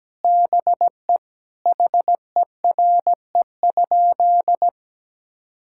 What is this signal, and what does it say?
Morse code, heard as a keyed tone, about 17 wpm, 700 Hz. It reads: BE HERE?
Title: Sentences from Top 1000 Words - Review Part 5 - 17wpm